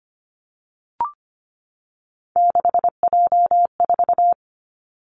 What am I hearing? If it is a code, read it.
Morse code, heard as a keyed tone, about 25 wpm, 700 Hz. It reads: 6J4
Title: Three Challenging Letter-Number Combinations 25wpm